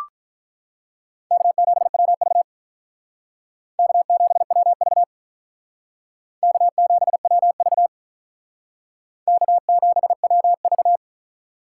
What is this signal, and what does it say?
Morse code, heard as a keyed tone, about 44 wpm, 700 Hz. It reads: K7WV K7WV K7WV K7WV